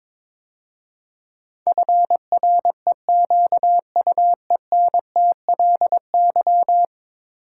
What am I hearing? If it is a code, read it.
Morse code, heard as a keyed tone, about 22 wpm, 700 Hz. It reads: FREQUENTLY
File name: practice / 22wpm / 10 Letter Words - First 3 Letters Given - 22wpm